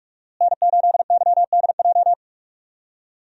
Morse code, heard as a keyed tone, about 46 wpm, 700 Hz. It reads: N8YDJ
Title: USA Call Signs - 5 Characters - Repeated 3x - 45wpm